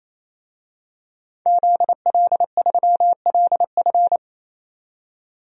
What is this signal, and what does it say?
Morse code, heard as a keyed tone, about 27 wpm, 700 Hz. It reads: ZL3LF
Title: DXCC-100 Call Signs - 5 Characters - 28wpm